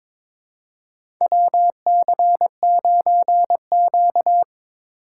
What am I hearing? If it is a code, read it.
Morse code, heard as a keyed tone, about 22 wpm, 700 Hz. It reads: WC9Q
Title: USA Call Signs - 4 Characters - 4x Character Spacing - 22wpm